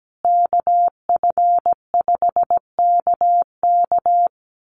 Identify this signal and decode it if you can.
Morse code, heard as a keyed tone, about 17 wpm, 700 Hz. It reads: KF5KK
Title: USA Call Signs Repeated 3x 17wpm